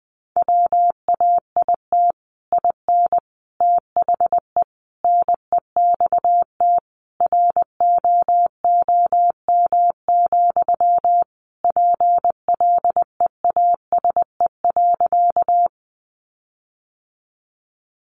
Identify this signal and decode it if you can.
Morse code, heard as a keyed tone, about 19 wpm, 700 Hz. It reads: WAIT IN THE NEXT ROOM, PLEASE.
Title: Sentences from Top 1500 Words 20wpm